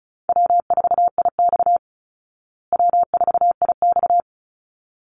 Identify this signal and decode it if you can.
Morse code, heard as a keyed tone, about 34 wpm, 700 Hz. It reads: W4IX W4IX
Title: Easy Call Signs Repeated 2x - 35wpm